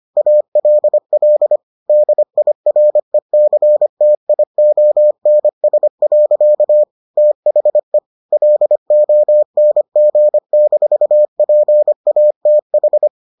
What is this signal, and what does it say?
Morse code, heard as a keyed tone, about 25 wpm, 600 Hz. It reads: ALL DIRECTIONS. THE LONG-PATH